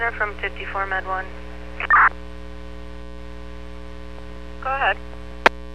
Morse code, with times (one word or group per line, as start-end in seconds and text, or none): none